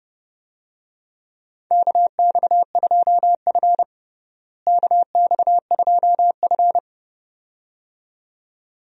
Morse code, heard as a keyed tone, about 30 wpm, 700 Hz. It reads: KX2F KX2F
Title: USA Call Signs - 4 Characters - Repeated 2x - 30wpm